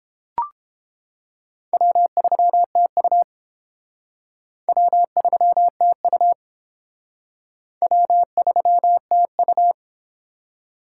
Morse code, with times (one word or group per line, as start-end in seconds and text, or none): 1.7-3.2 W3TU
4.7-6.3 W3TU
7.8-9.7 W3TU